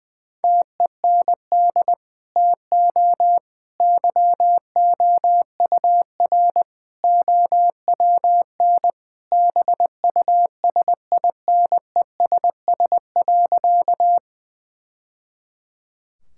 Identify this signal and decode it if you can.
Morse code, heard as a keyed tone, about 20 wpm, 700 Hz. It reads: TEND TO YOUR OWN BUSINESS.
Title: Sentences from Top 1000 Words - Review Part 5 - 20wpm